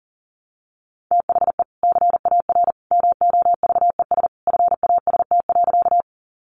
Morse code, heard as a keyed tone, about 40 wpm, 700 Hz. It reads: THE CAR MOVES FAST.
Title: Easy Phrases 40wpm